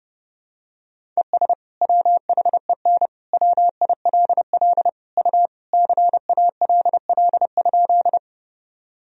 Morse code, heard as a keyed tone, about 30 wpm, 700 Hz. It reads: ES WHEN WILL U CALL?
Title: Sentences from Top 100 Words and Abbreviations 30wpm